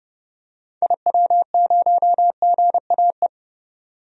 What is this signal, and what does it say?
Morse code, heard as a keyed tone, about 30 wpm, 700 Hz. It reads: IW0GAE
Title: DXCC-100 Call Signs - 6 Characters - 30wpm